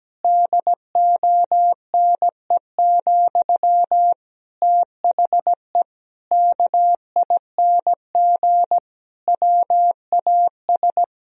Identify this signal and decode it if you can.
Morse code, heard as a keyed tone, about 17 wpm, 700 Hz. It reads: DONE, THE KING WAS
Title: Sentences from Top 1000 Words - Review Part 3 - 17wpm